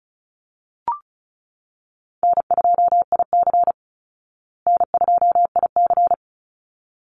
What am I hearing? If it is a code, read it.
Morse code, heard as a keyed tone, about 34 wpm, 700 Hz. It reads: N2IC N2IC
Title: Call Signs Repeated 3x - 35wpm